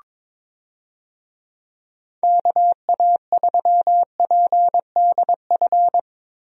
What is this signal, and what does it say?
Morse code, heard as a keyed tone, about 22 wpm, 700 Hz. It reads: KA3PDF
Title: USA Call Signs - 6 Characters - Repeated 3x - 22wpm